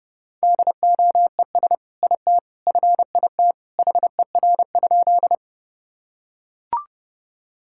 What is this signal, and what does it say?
Morse code, heard as a keyed tone, about 30 wpm, 700 Hz. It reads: DOES IT FIT HER?
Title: Sentences from Top 1000 Words - Review Part 5 - 30wpm